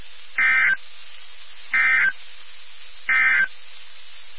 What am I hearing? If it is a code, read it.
Emergency Alert System (SAME) tones, heard as harsh buzzing bursts. It is heard from 0.3 to 3.6 s.
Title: NOAA Weather Radio WXK79